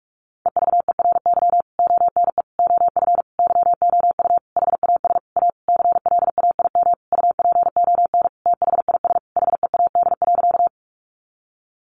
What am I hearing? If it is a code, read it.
Morse code, heard as a keyed tone, about 44 wpm, 700 Hz. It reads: EVERY ONE OF YOU HAS A CLAIM UPON THIS HEAD.
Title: Sentences from Top 1500 Words 45wpm